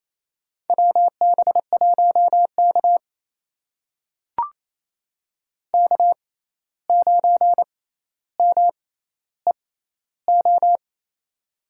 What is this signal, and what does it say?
Morse code, heard as a keyed tone, about 28 wpm, spaced as for 13 wpm, 700 Hz. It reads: WB1K K9MEO
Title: USA Call Signs - 6x Character Spacing - 28wpm